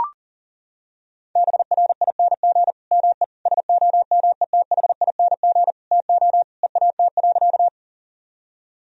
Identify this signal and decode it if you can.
Morse code, heard as a keyed tone, about 40 wpm, 700 Hz. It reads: BRING ME SOMETHING TO EAT.
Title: Sentences from Top 1000 Words - Review Part 3 - 40wpm